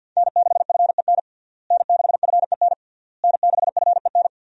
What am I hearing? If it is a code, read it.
Morse code, heard as a keyed tone, about 48 wpm, 700 Hz. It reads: N6FEN N6FEN N6FEN
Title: USA Call Signs - 5 Characters - Repeated 3x - 50wpm